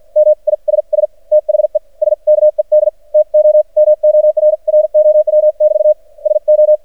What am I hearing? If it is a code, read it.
Morse code, heard as a keyed tone, about 49 wpm, spaced as for 41 wpm, 610 Hz. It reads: MISS THE SKED TO MORROW<BT> SO
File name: archived / CRSnet